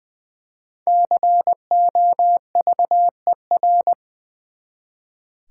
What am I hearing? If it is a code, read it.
Morse code, heard as a keyed tone, about 20 wpm, 700 Hz. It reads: COVER
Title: Five Letter Words - 20wpm